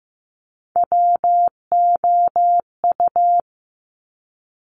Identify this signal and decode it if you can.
Morse code, heard as a keyed tone, about 15 wpm, 690 Hz. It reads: WOU
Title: Top Three Letter Combinations in Words 15wpm